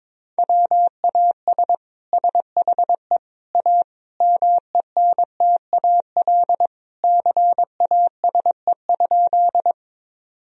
Morse code, heard as a keyed tone, about 22 wpm, 700 Hz. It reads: WAS SHE A MENTAL CASE?
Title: Sentences from Top 1000 Words - Review Part 4 - 22wpm